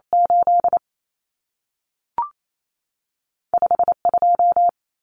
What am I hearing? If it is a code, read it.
Morse code, heard as a keyed tone, about 28 wpm, spaced as for 3 wpm, 700 Hz. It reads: 85 2
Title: Numbers - 2-Digit 28wpm